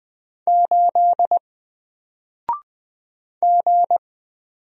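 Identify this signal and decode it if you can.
Morse code, heard as a keyed tone, about 20 wpm, 700 Hz. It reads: 8 G
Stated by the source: Single letter and number Morse drill at 20 wpm, from N4LCD.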